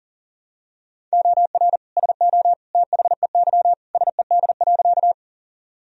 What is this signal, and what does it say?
Morse code, heard as a keyed tone, about 40 wpm, 700 Hz. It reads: OR SO THEY SED.
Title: Sentences from Top 100 Words and Abbreviations 40wpm